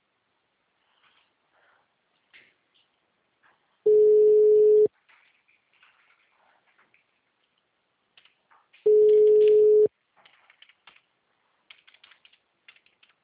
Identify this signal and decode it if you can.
Ringback tone (Europe, 425 Hz).
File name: out- -2016- - - .14180